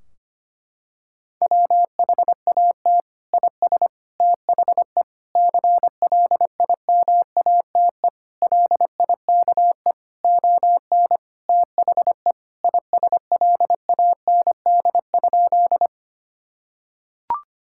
Morse code, heard as a keyed tone, about 25 wpm, 700 Hz. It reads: WHAT IS THE CLIMATE LIKE ON THE ISLAND?
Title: Sentences from Top 1600 Words 25wpm